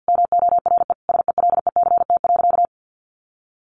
Morse code, heard as a keyed tone, about 50 wpm, 700 Hz. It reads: MORE SELECT.
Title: Sentences from Top 1300 Words 50wpm